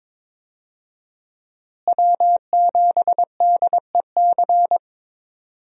Morse code, heard as a keyed tone, about 22 wpm, 700 Hz. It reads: W7DEC
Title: USA Call Signs - 5 Characters - Repeated 2x - 22wpm